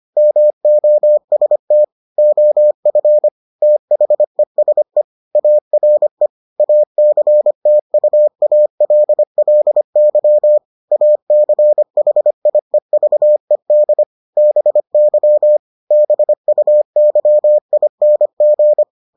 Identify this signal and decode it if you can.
Morse code, heard as a keyed tone, about 25 wpm, 600 Hz. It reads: MOST OF THESE ARE ACTUALLY ACHIEVED BY BUYING